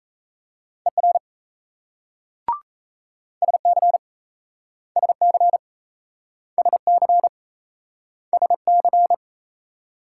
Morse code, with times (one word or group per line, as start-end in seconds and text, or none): 0.9-1.2 ER
3.4-4.0 SC
5.0-5.6 SC
6.6-7.3 SC
8.3-9.1 SC